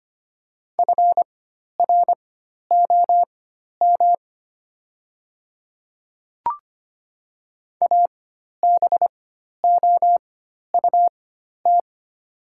Morse code, heard as a keyed tone, about 25 wpm, spaced as for 12 wpm, 700 Hz. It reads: FROM ABOUT
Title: 4x Character Spacing - Top 100 Words - 25wpm